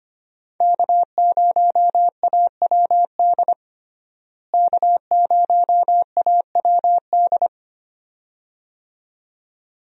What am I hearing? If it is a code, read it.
Morse code, heard as a keyed tone, about 25 wpm, 700 Hz. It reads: K0AWD K0AWD
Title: USA Call Signs - 5 Characters - Repeated 2x - 25wpm